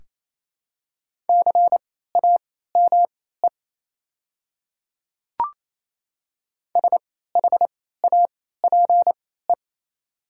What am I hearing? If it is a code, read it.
Morse code, heard as a keyed tone, about 28 wpm, spaced as for 16 wpm, 700 Hz. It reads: CAME SHAPE
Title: 3x Character Spacing - Top 500 Words - 28wpm